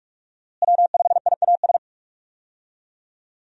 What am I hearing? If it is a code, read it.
Morse code, heard as a keyed tone, about 42 wpm, 700 Hz. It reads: W5IAS
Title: USA Call Signs - 5 Characters - Repeated 3x - 45wpm